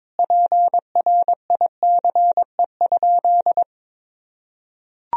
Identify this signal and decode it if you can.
Morse code, heard as a keyed tone, about 22 wpm, 700 Hz. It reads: PRICE?
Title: Sentences from Top 1000 Words - Review Part 3 - 22wpm